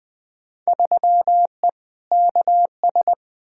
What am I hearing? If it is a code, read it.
Morse code, heard as a keyed tone, about 20 wpm, 700 Hz. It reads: 3E KS
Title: Field Day Exchanges Encore - 20wpm